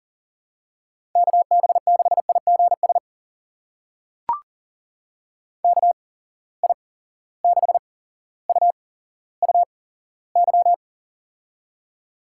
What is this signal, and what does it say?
Morse code, heard as a keyed tone, about 39 wpm, spaced as for 17 wpm, 700 Hz. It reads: KB6IGS KI6UUY
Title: USA Call Signs - 6 Characters - 8x Character Spacing - 40wpm